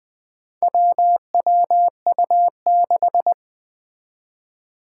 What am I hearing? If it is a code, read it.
Morse code, heard as a keyed tone, about 20 wpm, 700 Hz. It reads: WWU6
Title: Four Challenging Letter-Number Combinations 20wpm